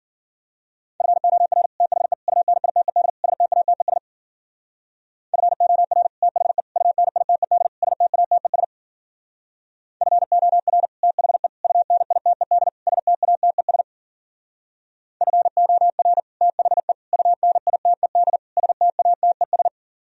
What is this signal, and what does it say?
Morse code, heard as a keyed tone, about 52 wpm, 700 Hz. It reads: FOR THE UNITED STATES FOR THE UNITED STATES FOR THE UNITED STATES FOR THE UNITED STATES